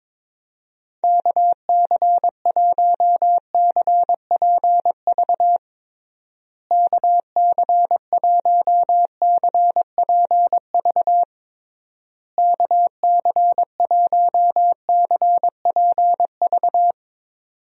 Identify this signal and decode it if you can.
Morse code, heard as a keyed tone, about 22 wpm, 700 Hz. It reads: KC1CPV KC1CPV KC1CPV
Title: USA Call Signs - 6 Characters - Repeated 3x - 22wpm